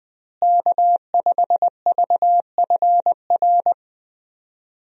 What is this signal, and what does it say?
Morse code, heard as a keyed tone, about 20 wpm, 700 Hz. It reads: K5VFR
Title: USA Call Signs - 5 Characters - Repeated 2x - 20wpm